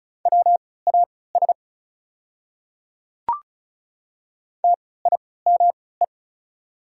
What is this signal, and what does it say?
Morse code, heard as a keyed tone, about 34 wpm, spaced as for 20 wpm, 690 Hz. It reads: WAS TIME
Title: 3x Character Spacing - Top 100 Words - 35wpm